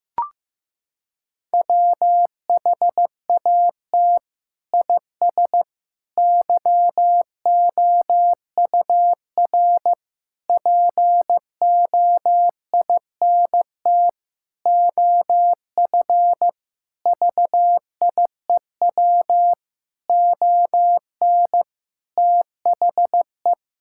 Morse code, heard as a keyed tone, about 14 wpm, 700 Hz. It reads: WHAT IS YOUR POINT OF VIEW ON THE
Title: Sentences from Top 1000 Words - Review Part 6 - 15wpm